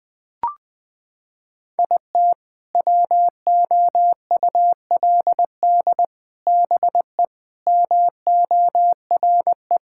Morse code, heard as a keyed tone, about 20 wpm, 700 Hz. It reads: IT WOULD BE MORE
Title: Sets of 8 Words - 20wpm